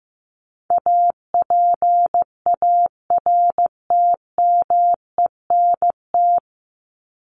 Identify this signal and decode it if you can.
Morse code, heard as a keyed tone, about 15 wpm, 700 Hz. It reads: APARTMENT
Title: Top 751-1000 Words 15wpm